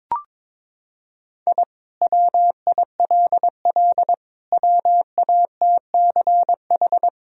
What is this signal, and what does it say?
Morse code, heard as a keyed tone, about 21 wpm, 700 Hz. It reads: I WILL WATCH
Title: Sentences from Top 1000 Words - Review Part 1 - 22wpm